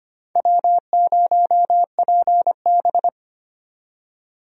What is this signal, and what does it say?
Morse code, heard as a keyed tone, about 25 wpm, 700 Hz. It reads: W0PB
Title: USA Call Signs - 4 Characters - 25wpm